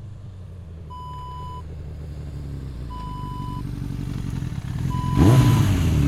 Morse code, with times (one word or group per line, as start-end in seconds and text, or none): none